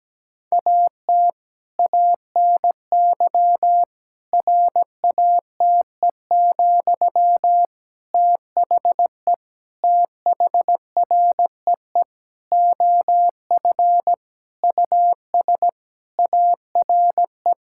Morse code, heard as a keyed tone, about 17 wpm, 700 Hz. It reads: AT ANY RATE, THE THREE OF US ARE
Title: Sentences from Top 1000 Words - Review Part 6 - 17wpm